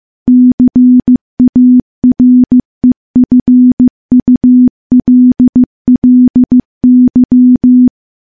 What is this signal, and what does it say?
Morse code, heard as a keyed tone, about 15 wpm, 250 Hz. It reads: CAREFULLY
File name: morse / morse-training-15WPM